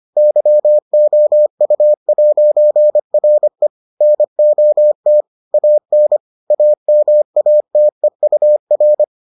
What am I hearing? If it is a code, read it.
Morse code, heard as a keyed tone, about 25 wpm, 600 Hz. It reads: YOU'RE NOT AN AMATEUR